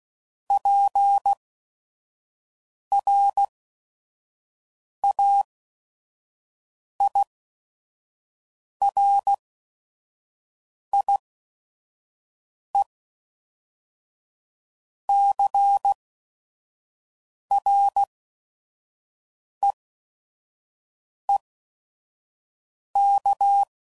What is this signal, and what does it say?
Morse code, heard as a keyed tone, about 16 wpm, spaced as for 5 wpm, 800 Hz. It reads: PRAIRIECREEK
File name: CW Code Groups @5WPM / fromSM's / Cities